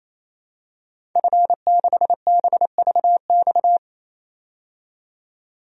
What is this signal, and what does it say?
Morse code, heard as a keyed tone, about 28 wpm, 700 Hz. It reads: F6BVX
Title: DXCC-100 Call Signs - 5 Characters - 28wpm